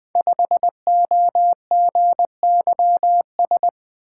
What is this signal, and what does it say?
Morse code, heard as a keyed tone, about 20 wpm, 700 Hz. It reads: 5OGYS
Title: Code Groups - 5 Letter:Numbers - 20wpm